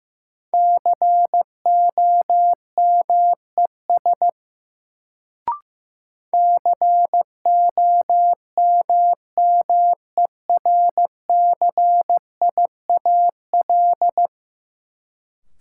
Morse code, heard as a keyed tone, about 15 wpm, 700 Hz. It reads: COMES COMMERCIAL
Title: Sentences from Top 1000 Words - Review Part 5 - New Word Drill - 15wpm